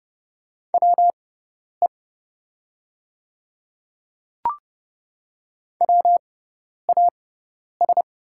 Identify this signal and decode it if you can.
Morse code, heard as a keyed tone, about 30 wpm, spaced as for 10 wpm, 700 Hz. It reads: WE WAS
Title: 6x Character Spacing - Top 100 Words - 30wpm